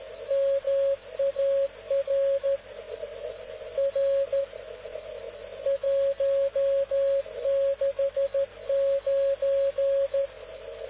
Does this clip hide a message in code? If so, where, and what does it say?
Yes: Morse code from 0.3 to 10.3 s, heard as a keyed tone, about 13 wpm, 550 Hz, with radio noise. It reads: MAR R 169